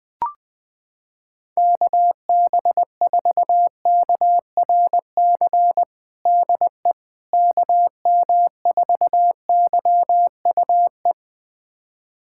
Morse code, heard as a keyed tone, about 20 wpm, 700 Hz. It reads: KB4KRC DE KM4YUE
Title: USA Callsign DE Callsign - 6 Characters - 20wpm